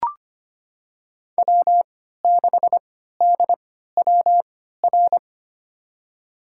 Morse code, heard as a keyed tone, about 25 wpm, spaced as for 14 wpm, 700 Hz. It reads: W6DWR
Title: USA Call Signs - 5 Characters - 3x Character Spacing - 25wpm